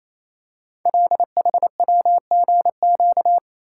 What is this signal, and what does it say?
Morse code, heard as a keyed tone, about 27 wpm, 700 Hz. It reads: LHWGQ
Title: Code Groups - 5 Letters - 28wpm